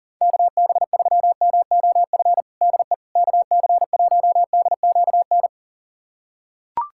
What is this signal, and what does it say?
Morse code, heard as a keyed tone, about 40 wpm, 700 Hz. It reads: KB3MOF DE KC1DQN
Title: USA Callsign DE Callsign 40wpm